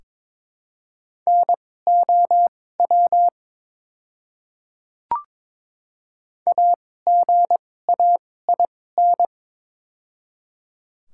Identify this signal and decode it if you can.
Morse code, heard as a keyed tone, about 22 wpm, spaced as for 16 wpm, 700 Hz. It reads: NOW AGAIN
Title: 2x Character Spacing - Top 300 Words - 22wpm